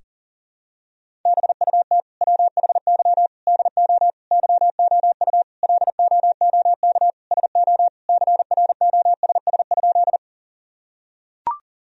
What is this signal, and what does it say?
Morse code, heard as a keyed tone, about 37 wpm, 700 Hz. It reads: BUT WHY DO YOU LOOK SO CROSS?
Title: Sentences from Top 1000 Words - Review Part 1 - 40wpm